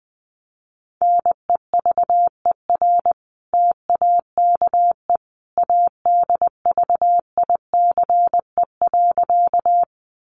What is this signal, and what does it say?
Morse code, heard as a keyed tone, about 20 wpm, 700 Hz. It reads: NEVER TAKE ADVICE.
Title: Sentences from Top 1100 Words 20wpm